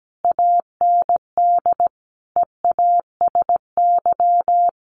Morse code, heard as a keyed tone, about 17 wpm, 700 Hz. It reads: AND EASY
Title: Sentences from Top 1400 Words 17wpm